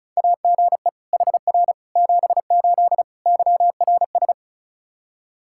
Morse code, heard as a keyed tone, about 36 wpm, 700 Hz. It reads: AGE HR 78 YRS
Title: Practice QSO - Exchange 3 - More Info - 35wpm